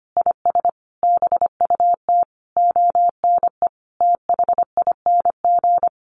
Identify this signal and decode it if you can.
Morse code, heard as a keyed tone, about 25 wpm, 700 Hz. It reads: IS BUT ONE THING
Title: Sentences from Top 100 Words 25wpm